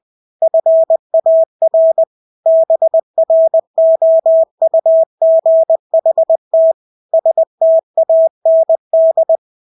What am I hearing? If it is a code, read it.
Morse code, heard as a keyed tone, about 20 wpm, 650 Hz. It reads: FAR BROUGHT STAND
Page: Engelske ord 20 wpm